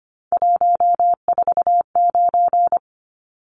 Morse code, heard as a keyed tone, about 25 wpm, 700 Hz. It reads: 149
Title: Numbers - 3-Digit 25wpm